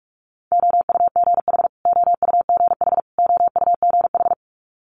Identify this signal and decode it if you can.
Morse code, heard as a keyed tone, about 46 wpm, 700 Hz. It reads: OUGH OUGH OUGH
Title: Multi-letter Phonemes Repeated 3x 45wpm